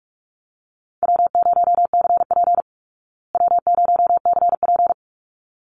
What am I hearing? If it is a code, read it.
Morse code, heard as a keyed tone, about 44 wpm, 700 Hz. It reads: W0CP W0CP
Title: Easy Call Signs Repeated 2x - 45wpm